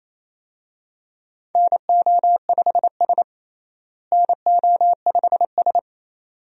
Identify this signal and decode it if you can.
Morse code, heard as a keyed tone, about 28 wpm, 700 Hz. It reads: NO5S NO5S
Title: USA Call Signs - 4 Characters - Repeated 2x - 28wpm